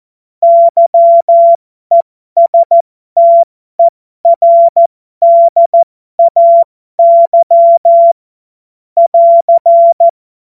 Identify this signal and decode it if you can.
Morse code, heard as a keyed tone, about 14 wpm, spaced as for 12 wpm, 680 Hz. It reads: YESTERDAY <AR>
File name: Morsekurs / Witze